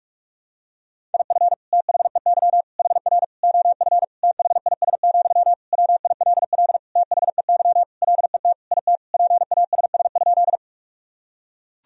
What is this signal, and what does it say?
Morse code, heard as a keyed tone, about 41 wpm, 700 Hz. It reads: IF THEY HR OF THIS, WILL THEY LET IT PASS?
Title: Sentences from Top 400 Words and Abbreviations 45wpm